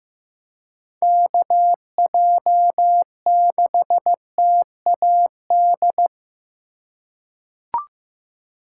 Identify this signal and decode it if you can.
Morse code, heard as a keyed tone, about 16 wpm, 700 Hz. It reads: KJ6TAD
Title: USA Call Signs - 6 Characters - 4x Character Spacing - 15wpm